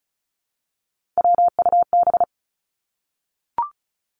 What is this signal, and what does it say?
Morse code, heard as a keyed tone, about 36 wpm, 700 Hz. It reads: WUB
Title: Three Challenging Letter-Number Combinations 35wpm